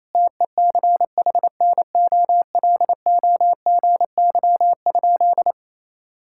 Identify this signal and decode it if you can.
Morse code, heard as a keyed tone, about 28 wpm, 700 Hz. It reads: TECHNOLOGY?
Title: Sentences from Top 1000 Words - Review Part 5 - 28wpm